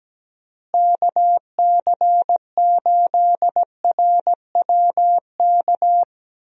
Morse code, heard as a keyed tone, about 17 wpm, 700 Hz. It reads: KC8RWK
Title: USA Call Signs - 6 Characters - Repeated 2x - 17wpm